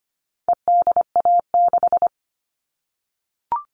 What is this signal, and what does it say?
Morse code, heard as a keyed tone, about 25 wpm, 700 Hz. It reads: EDA6
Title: All Cut Numbers - 4-Digit 25wpm